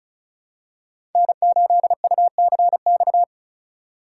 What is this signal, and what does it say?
Morse code, heard as a keyed tone, about 36 wpm, 700 Hz. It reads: N8UCX
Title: USA Call Signs - 4x Character Spacing - 35wpm